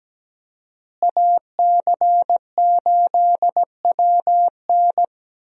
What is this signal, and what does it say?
Morse code, heard as a keyed tone, about 17 wpm, 700 Hz. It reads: AC8WN
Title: USA Call Signs - 5 Characters - Repeated 2x - 17wpm